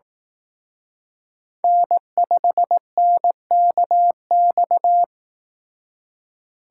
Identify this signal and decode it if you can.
Morse code, heard as a keyed tone, about 18 wpm, 700 Hz. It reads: N5NKX
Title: Speed Racing - USA 1x3 Call Signs 15wpm